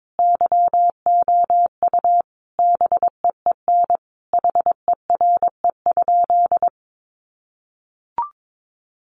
Morse code, heard as a keyed tone, about 22 wpm, 700 Hz. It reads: YOU BEEN HERE?
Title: Sentences from Top 200 Words 22wpm